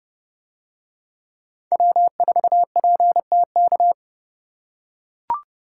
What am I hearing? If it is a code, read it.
Morse code, heard as a keyed tone, about 28 wpm, 700 Hz. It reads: W4PTK